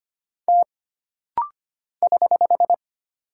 Morse code, heard as a keyed tone, about 25 wpm, 700 Hz. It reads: T <HH>
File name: Single Character (Letter:Number:Punctuation:Prosign) - Rapid-Fire - 25wpm